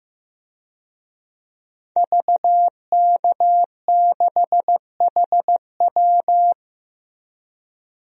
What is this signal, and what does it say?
Morse code, heard as a keyed tone, about 15 wpm, 700 Hz. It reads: VK6HW
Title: DXCC-100 Call Signs - 5 Characters - Repeated 2x - 15wpm